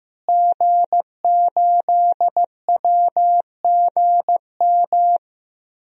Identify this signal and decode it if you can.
Morse code, heard as a keyed tone, about 15 wpm, 700 Hz. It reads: G8WGM
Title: DXCC-100 Call Signs - 5 Characters - Repeated 3x - 15wpm